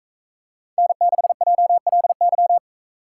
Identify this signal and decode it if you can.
Morse code, heard as a keyed tone, about 41 wpm, 700 Hz. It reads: N6JLY